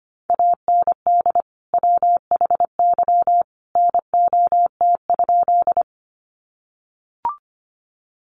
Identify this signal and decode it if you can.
Morse code, heard as a keyed tone, about 25 wpm, 700 Hz. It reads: AND WHY NOT?
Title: Sentences from Top 600 Words 25wpm